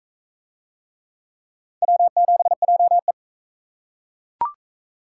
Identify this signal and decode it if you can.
Morse code, heard as a keyed tone, about 40 wpm, 700 Hz. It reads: W7JE